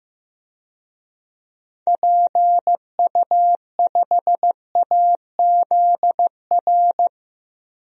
Morse code, heard as a keyed tone, about 15 wpm, 700 Hz. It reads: PU5AZR
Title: DXCC-100 Call Signs - 6 Characters - Repeated 3x - 15wpm